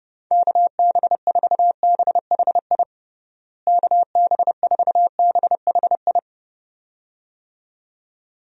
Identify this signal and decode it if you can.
Morse code, heard as a keyed tone, about 30 wpm, 700 Hz. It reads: KB4BHI KB4BHI
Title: USA Call Signs - 6 Characters - Repeated 2x - 30wpm